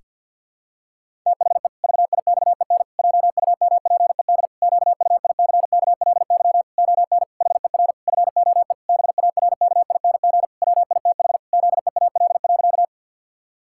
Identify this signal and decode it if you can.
Morse code, heard as a keyed tone, about 52 wpm, 700 Hz. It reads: THE VIXEN JUMPED QUICKLY ON HER FOE BARKING WITH ZEAL.
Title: Pangram Challenge v2 - 50wpm